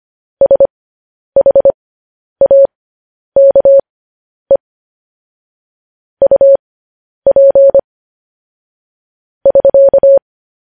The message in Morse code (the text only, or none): SHAKE UP <SK>